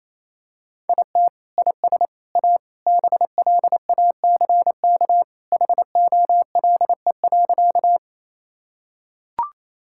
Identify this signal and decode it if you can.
Morse code, heard as a keyed tone, about 28 wpm, 700 Hz. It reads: IT IS A BLACK HOLE.
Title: Sentences from Top 1600 Words 28wpm